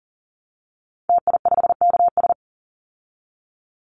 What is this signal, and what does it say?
Morse code, heard as a keyed tone, about 40 wpm, 700 Hz. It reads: TI5KS
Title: Call Signs Repeated 3x - 40wpm